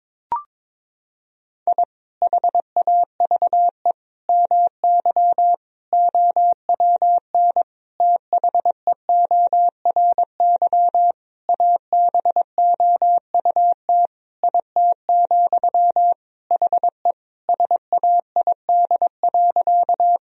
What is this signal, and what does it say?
Morse code, heard as a keyed tone, about 22 wpm, 700 Hz. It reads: I HAVE MY OWN THEORY ABOUT IT, HE SAID.
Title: Sentences from Top 1000 Words - Review Part 3 - 22wpm